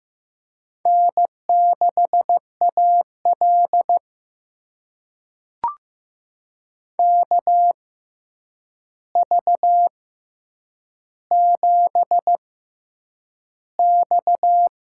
Morse code, heard as a keyed tone, about 15 wpm, spaced as for 8 wpm, 700 Hz. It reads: N6AL KV7X
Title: USA Call Signs - 4 Characters - 6x Character Spacing - 15wpm